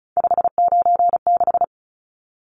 Morse code, heard as a keyed tone, about 34 wpm, 700 Hz. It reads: 596
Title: Numbers - 3-Digit 35wpm